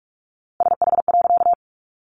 Morse code, heard as a keyed tone, about 44 wpm, 700 Hz. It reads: SH.
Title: A Sound Of Thunder 45wpm